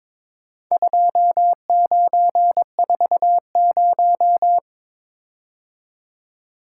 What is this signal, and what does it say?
Morse code, heard as a keyed tone, about 22 wpm, 700 Hz. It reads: 2940
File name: practice / 22wpm / Four Letter-Number Combinations - 22wpm